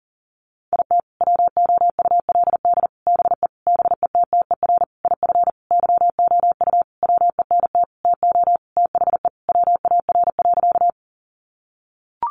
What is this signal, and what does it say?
Morse code, heard as a keyed tone, about 40 wpm, 700 Hz. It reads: IT WOULD BE BETTER IF YOU WENT TO THE WAR.
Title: Sentences from Top 400 Words 40wpm